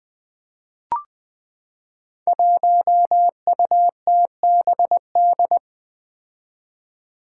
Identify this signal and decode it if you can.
Morse code, heard as a keyed tone, about 20 wpm, 700 Hz. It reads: 1UTBD
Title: Five Challenging Letter-Number Combinations 20wpm